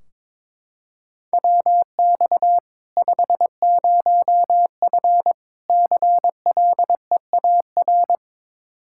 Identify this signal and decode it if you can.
Morse code, heard as a keyed tone, about 22 wpm, 700 Hz. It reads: WX 50F CLEAR
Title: K1USN SST QSO Practice 22wpm